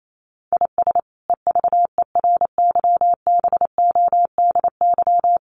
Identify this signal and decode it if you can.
Morse code, heard as a keyed tone, about 28 wpm, 700 Hz. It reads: IS EVERYBODY
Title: Sentences from Top 1400 Words 28wpm